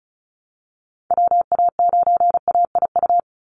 Morse code, heard as a keyed tone, about 34 wpm, 700 Hz. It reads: WA9AIU
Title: USA Call Signs 35wpm